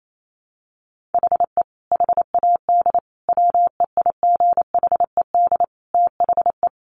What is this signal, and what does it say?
Morse code, heard as a keyed tone, about 28 wpm, 700 Hz. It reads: HE HAD WEIGHED THE
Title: A Sound Of Thunder 28wpm